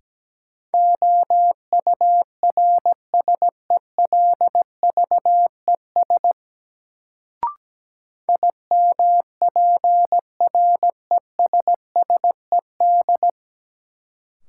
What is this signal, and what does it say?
Morse code, heard as a keyed tone, about 17 wpm, 700 Hz. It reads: OURSELVES IMPRESSED
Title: Sentences from Top 1600 Words - New Word Drill - 17wpm